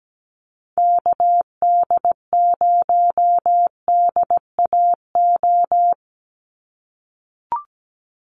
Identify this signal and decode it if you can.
Morse code, heard as a keyed tone, about 17 wpm, 700 Hz. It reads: KD0DAO
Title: USA Call Signs 17wpm